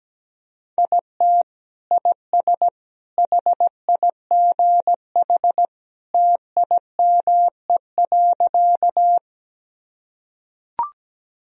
Morse code, heard as a keyed tone, about 17 wpm, 700 Hz. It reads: IT IS HIGH TIME.
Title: Sentences from Top 1000 Words - Review Part 1 - 17wpm